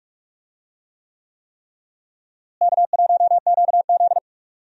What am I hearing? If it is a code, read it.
Morse code, heard as a keyed tone, about 48 wpm, 700 Hz. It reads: K1QZ
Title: USA Call Signs - 4 Characters - Repeated 3x - 45wpm